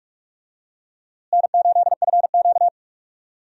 Morse code, heard as a keyed tone, about 41 wpm, 700 Hz. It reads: N8FQ
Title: USA Call Signs - 4 Characters - Repeated 3x - 45wpm